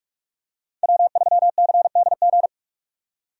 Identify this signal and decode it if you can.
Morse code, heard as a keyed tone, about 44 wpm, 700 Hz. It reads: W3CDG
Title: USA Call Signs - 5 Characters - 6x Character Spacing - 45wpm